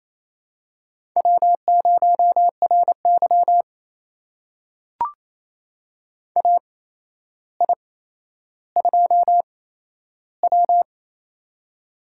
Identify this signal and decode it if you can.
Morse code, heard as a keyed tone, about 28 wpm, spaced as for 12 wpm, 700 Hz. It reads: W0RY AI2W
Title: USA Call Signs - 4 Characters - 8x Character Spacing - 28wpm